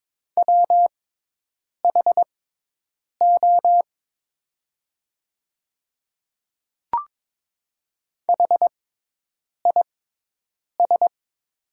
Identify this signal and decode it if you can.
Morse code, heard as a keyed tone, about 22 wpm, spaced as for 8 wpm, 700 Hz. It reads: WHO HIS
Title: 6x Character Spacing - Top 100 Words - 22wpm